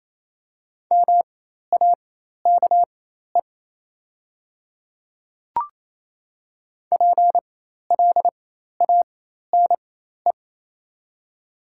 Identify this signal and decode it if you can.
Morse code, heard as a keyed tone, about 27 wpm, spaced as for 13 wpm, 700 Hz. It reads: MAKE PLANE
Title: 4x Character Spacing - Top 500 Words - 28wpm